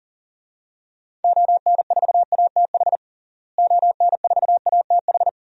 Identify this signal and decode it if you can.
Morse code, heard as a keyed tone, about 40 wpm, 700 Hz. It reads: ON4ATH ON4ATH
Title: DXCC-100 Call Signs - 6 Characters - Repeated 2x - 40wpm